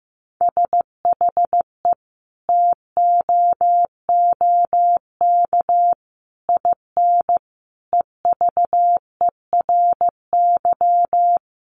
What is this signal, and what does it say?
Morse code, heard as a keyed tone, about 15 wpm, 700 Hz. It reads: SHE TOOK IN EVERY
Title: Sentences from Top 1500 Words 15wpm